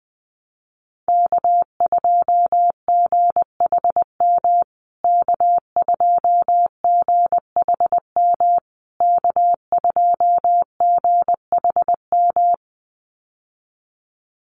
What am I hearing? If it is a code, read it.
Morse code, heard as a keyed tone, about 20 wpm, 700 Hz. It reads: K2GHM K2GHM K2GHM
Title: USA Call Signs Repeated 3x 20wpm